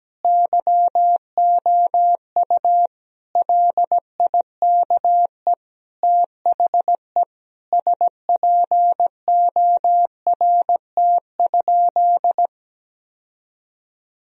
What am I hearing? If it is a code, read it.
Morse code, heard as a keyed tone, about 17 wpm, 700 Hz. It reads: YOU LIKE THE SPORT?
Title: Sentences from Top 1000 Words - Review Part 4 - 17wpm